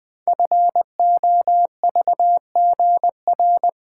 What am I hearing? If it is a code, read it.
Morse code, heard as a keyed tone, about 20 wpm, 700 Hz. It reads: FOVGR
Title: Code Groups - 5 Letters - 20wpm